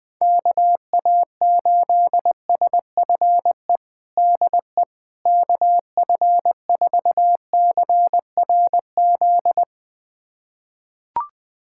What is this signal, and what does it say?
Morse code, heard as a keyed tone, about 20 wpm, 700 Hz. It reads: KA8SFE DE KF4CRZ
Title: USA Callsign DE Callsign - 6 Characters - 20wpm